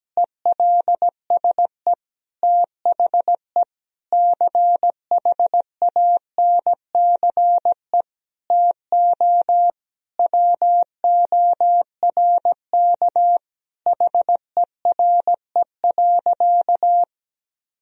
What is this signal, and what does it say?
Morse code, heard as a keyed tone, about 17 wpm, 700 Hz. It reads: ELSE THE CHANCE TO WORK HERE.
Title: Sentences from Top 1000 Words - Review Part 4 - 17wpm